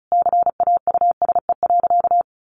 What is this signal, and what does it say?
Morse code, heard as a keyed tone, about 34 wpm, 700 Hz. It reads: CAUSE.
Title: Sentences from Top 700 Words 35wpm